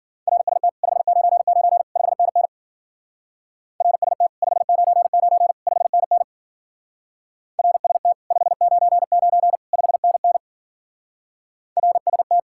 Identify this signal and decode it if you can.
Morse code, heard as a keyed tone, about 47 wpm, 700 Hz. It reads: RST 599 5NN RST 599 5NN RST 599 5NN RST
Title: Speed-Racing Common QSO Phrases - 40wpm